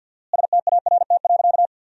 Morse code, heard as a keyed tone, about 50 wpm, 700 Hz. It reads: START.
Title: 4x Word-Spacing - Sentences from Top 300 Words - 50wpm